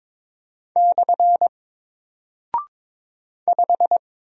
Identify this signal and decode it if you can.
Morse code, heard as a keyed tone, about 22 wpm, 700 Hz. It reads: / 5
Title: Single Character (Letter:Number:Punctuation:Prosign) Alt 22wpm